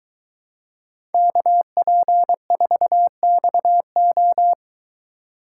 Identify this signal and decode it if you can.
Morse code, heard as a keyed tone, about 22 wpm, 700 Hz. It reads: KP4XO